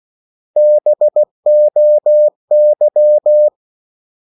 Morse code, heard as a keyed tone, about 16 wpm, 600 Hz. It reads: BOY
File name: CW / Practice / 500MostCommonEnglishWords16